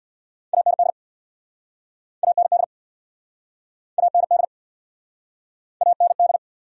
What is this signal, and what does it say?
Morse code, heard as a keyed tone, about 67 wpm, 700 Hz. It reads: AND AND AND AND